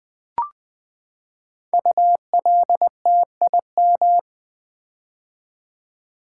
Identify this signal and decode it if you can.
Morse code, heard as a keyed tone, about 20 wpm, 700 Hz. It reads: ULTIM
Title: Build A Word - 5 Letters At A Time - 10-Letter Words - 20wpm